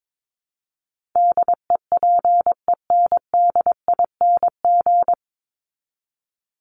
Morse code, heard as a keyed tone, about 22 wpm, 700 Hz. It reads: DEPENDING
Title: ING Suffix - Top 100 Words with Base Words 22wpm